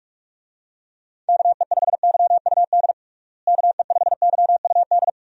Morse code, heard as a keyed tone, about 41 wpm, 700 Hz. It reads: KE5YUD KE5YUD
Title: USA Call Signs - 6 Characters - Repeated 2x - 45wpm